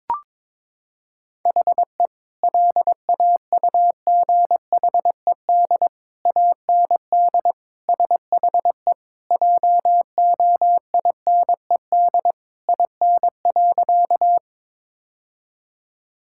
Morse code, heard as a keyed tone, about 21 wpm, 700 Hz. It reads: HE LAUGHED AND SHE JOINED IN.
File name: practice / 22wpm / Sentences with 2-Word Phrases - 22wpm